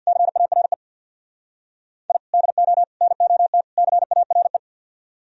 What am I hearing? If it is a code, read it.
Morse code, heard as a keyed tone, about 61 wpm, 700 Hz. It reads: CARE I DO NOT CARE
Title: Speed-Racing - Sets of 4 Words 50wpm